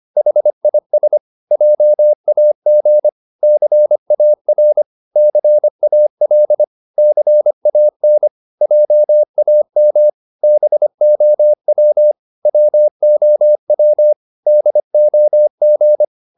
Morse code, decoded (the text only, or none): HIS JAG CAR CAL CAN JAM BOW WOW DOG